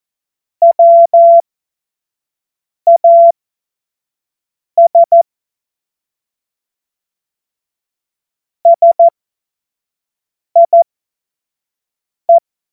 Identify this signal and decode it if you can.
Morse code, heard as a keyed tone, about 14 wpm, spaced as for 5 wpm, 680 Hz. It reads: WAS SIE
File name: Morsekurs / Witze / Witz 19 März 2024 25 F70